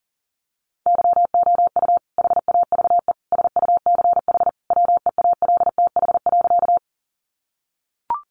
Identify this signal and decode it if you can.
Morse code, heard as a keyed tone, about 40 wpm, 700 Hz. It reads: YOU HAVE SUCH WEALTH.
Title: Sentences from Top 1200 Words 40wpm